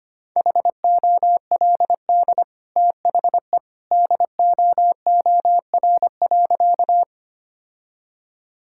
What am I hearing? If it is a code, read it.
Morse code, heard as a keyed tone, about 25 wpm, 700 Hz. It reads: HOLD THE DOOR.
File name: Easy Sentences with Hints - 25wpm